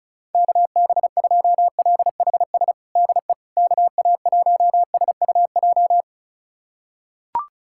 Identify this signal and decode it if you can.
Morse code, heard as a keyed tone, about 36 wpm, 700 Hz. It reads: KB2LHS DE KA1SUJ
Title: USA Callsign DE Callsign 35wpm